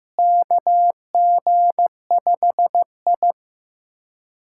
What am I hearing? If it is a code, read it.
Morse code, heard as a keyed tone, about 15 wpm, 700 Hz. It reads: KG5I
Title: USA Callsign DE Callsign - 4 Characters - 15wpm